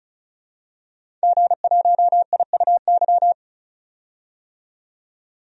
Morse code, heard as a keyed tone, about 36 wpm, 700 Hz. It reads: G1IUY
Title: DXCC-100 Call Signs - 5 Characters - 35wpm